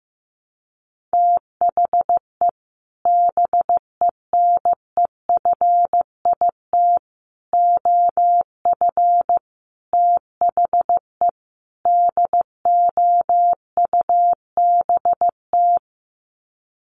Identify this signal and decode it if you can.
Morse code, heard as a keyed tone, about 15 wpm, 700 Hz. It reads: THE BENEFIT OF THE DOUBT
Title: Sets of 5 Words Encore 15wpm